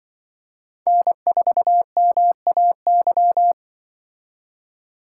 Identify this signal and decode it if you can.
Morse code, heard as a keyed tone, about 23 wpm, 700 Hz. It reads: N4MAY
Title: Speed Racing - USA 1x3 Call Signs 20wpm